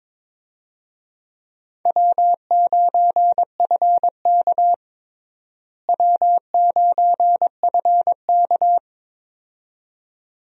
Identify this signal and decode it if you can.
Morse code, heard as a keyed tone, about 22 wpm, 700 Hz. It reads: W9FK W9FK
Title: USA Call Signs - 4 Characters - Repeated 2x - 22wpm